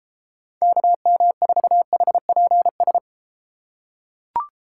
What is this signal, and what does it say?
Morse code, heard as a keyed tone, about 32 wpm, 700 Hz. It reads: KM4HPS